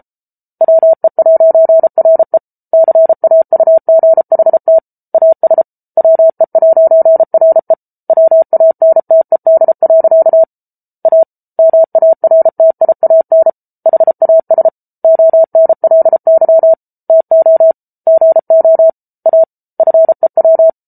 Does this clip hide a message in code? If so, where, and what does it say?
Yes: Morse code from 0.6 to 20.7 s, heard as a keyed tone, about 34 wpm, 650 Hz. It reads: WE'RE CAUGHT AS WE'RE WANTED. A MARTIAN HAS ONLY TO GO A FEW